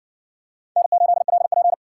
Morse code, heard as a keyed tone, about 60 wpm, 700 Hz. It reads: N8LP